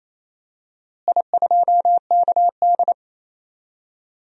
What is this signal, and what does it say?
Morse code, heard as a keyed tone, about 28 wpm, 700 Hz. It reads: I2KD
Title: DXCC-100 Call Signs - 4 Characters - Repeated 2x - 28wpm